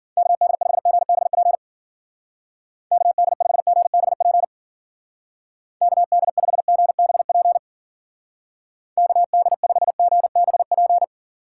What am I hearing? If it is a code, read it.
Morse code, heard as a keyed tone, about 52 wpm, 700 Hz. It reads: KD5GBP KD5GBP KD5GBP KD5GBP